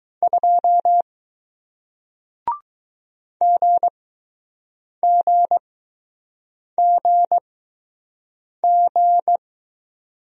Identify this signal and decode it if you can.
Morse code, heard as a keyed tone, about 21 wpm, spaced as for 6 wpm, 700 Hz. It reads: 2 GGGG